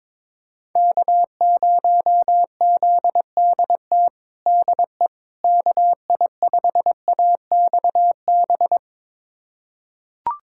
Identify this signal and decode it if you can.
Morse code, heard as a keyed tone, about 22 wpm, 700 Hz. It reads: K0ZDT DE KI5AXB
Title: USA Callsign DE Callsign 22wpm